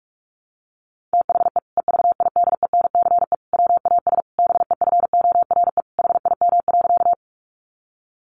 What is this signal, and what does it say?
Morse code, heard as a keyed tone, about 44 wpm, 700 Hz. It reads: THE EVIDENCE WAS BEFORE HIM.
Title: Sentences from Top 1300 Words 45wpm